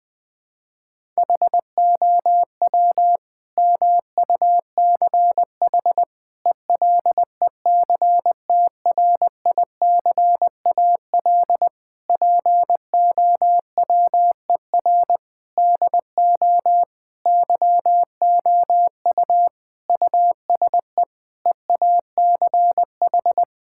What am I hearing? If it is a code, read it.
Morse code, heard as a keyed tone, about 20 wpm, 700 Hz. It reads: HOW MUCH ELECTRICAL POWER DO YOU USE EACH
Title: Sentences from Top 1000 Words - Review Part 6 - 20wpm